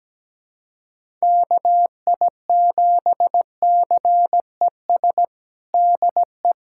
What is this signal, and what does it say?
Morse code, heard as a keyed tone, about 17 wpm, 700 Hz. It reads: KI7CES DE
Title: USA Callsign DE Callsign 17wpm